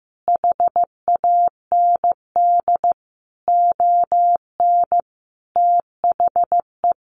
Morse code, heard as a keyed tone, about 15 wpm, 700 Hz. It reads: HAND ON THE
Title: Easy Phrases 15wpm